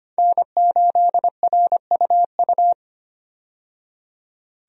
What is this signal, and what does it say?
Morse code, heard as a keyed tone, about 25 wpm, 700 Hz. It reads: N8RUU
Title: USA Call Signs - 5 Characters - 25wpm